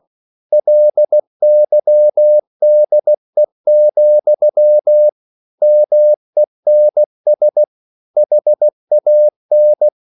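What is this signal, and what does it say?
Morse code, heard as a keyed tone, about 16 wpm, 600 Hz. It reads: LYDE, MENS HAN
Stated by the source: Slow (QRS) Morse code at 16 wpm, from CW med Gnister.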